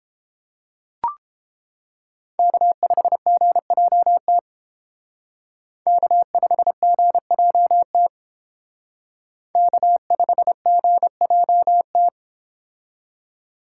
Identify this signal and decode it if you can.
Morse code, heard as a keyed tone, about 31 wpm, 700 Hz. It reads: K5GJT K5GJT K5GJT